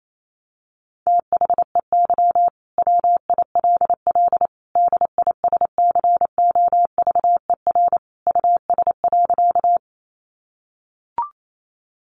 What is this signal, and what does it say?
Morse code, heard as a keyed tone, about 28 wpm, 700 Hz. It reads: THEY WILL DISCOVER US.
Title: Sentences from Top 1100 Words 28wpm